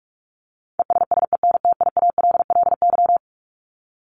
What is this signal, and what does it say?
Morse code, heard as a keyed tone, about 44 wpm, 700 Hz. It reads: ESSENTIALLY
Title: LY Suffix - Top 100 Words with Base Words 45wpm